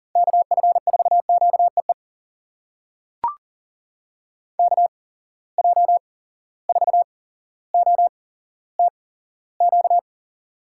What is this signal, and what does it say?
Morse code, heard as a keyed tone, about 38 wpm, spaced as for 17 wpm, 700 Hz. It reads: KF4QEE KJ4OTQ
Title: USA Call Signs - 6 Characters - 8x Character Spacing - 40wpm